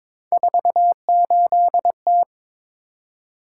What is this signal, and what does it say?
Morse code, heard as a keyed tone, about 22 wpm, 700 Hz. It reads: 48T
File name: practice / 22wpm / Three Letter-Number Combinations - 22wpm